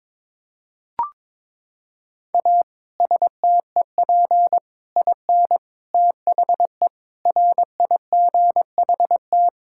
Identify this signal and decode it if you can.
Morse code, heard as a keyed tone, about 21 wpm, 700 Hz. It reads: A STEP IN THE RIGHT
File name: practice / 22wpm / Sets of 6 Words - 22wpm